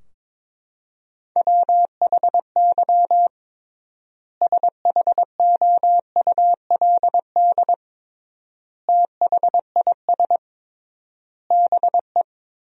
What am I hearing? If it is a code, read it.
Morse code, heard as a keyed tone, about 22 wpm, 700 Hz. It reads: WHY SHOULD THIS BE
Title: 3x Word-Spacing - Sentences from Top 300 Words - 22wpm